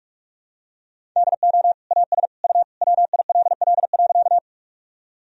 Morse code, heard as a keyed tone, about 46 wpm, 700 Hz. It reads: DO AS U WILL.
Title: Sentences from Top 100 Words and Abbreviations 45wpm